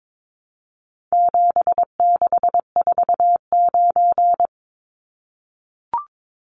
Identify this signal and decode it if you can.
Morse code, heard as a keyed tone, about 22 wpm, 700 Hz. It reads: 7649
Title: Numbers - 4-Digit 22wpm